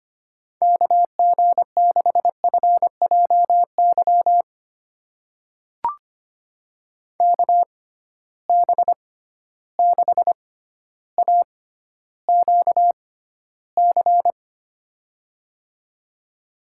Morse code, heard as a keyed tone, about 25 wpm, spaced as for 13 wpm, 700 Hz. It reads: KG6FJY KB6AQC
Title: USA Call Signs - 6 Characters - 6x Character Spacing - 25wpm